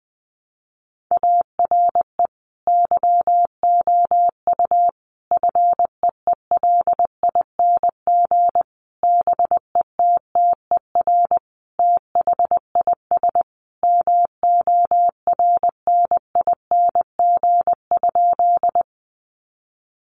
Morse code, heard as a keyed tone, about 20 wpm, 700 Hz. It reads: ARE YOU FEELING BETTER THIS MORNING?
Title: Sentences from Top 1200 Words 20wpm